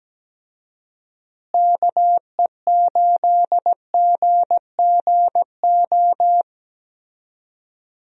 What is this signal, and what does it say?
Morse code, heard as a keyed tone, about 17 wpm, 700 Hz. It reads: KE8GGO
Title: USA Call Signs - 6 Characters - Repeated 2x - 17wpm